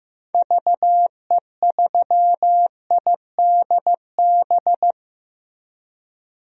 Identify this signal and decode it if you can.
Morse code, heard as a keyed tone, about 16 wpm, 700 Hz. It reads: VE3IDB
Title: DXCC-100 Call Signs - 6 Characters - Repeated 2x - 15wpm